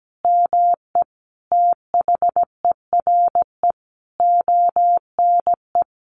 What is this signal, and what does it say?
Morse code, heard as a keyed tone, about 17 wpm, 700 Hz. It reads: ME THERE ONE
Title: Sentences from Top 300 Words 17wpm